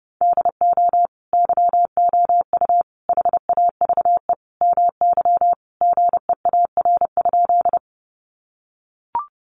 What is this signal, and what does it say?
Morse code, heard as a keyed tone, about 30 wpm, 700 Hz. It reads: DO YOU HAVE MY GEAR?
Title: Sentences from Top 1400 Words 30wpm